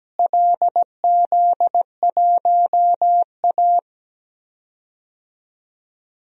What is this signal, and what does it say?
Morse code, heard as a keyed tone, about 17 wpm, 700 Hz. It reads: LZ1A
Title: DXCC-100 Call Signs - 4 Characters - Repeated 3x - 17wpm